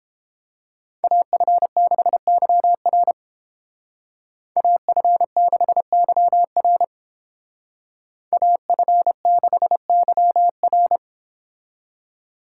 Morse code, heard as a keyed tone, about 29 wpm, 700 Hz. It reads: AF6YR AF6YR AF6YR